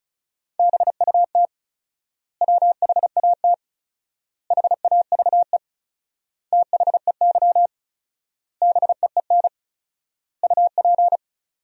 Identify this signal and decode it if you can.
Morse code, heard as a keyed tone, about 36 wpm, 700 Hz. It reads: BUT WHAT HAVE THEY BEEN UP
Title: 4x Word-Spacing - Sentences from Top 100 Words - 35wpm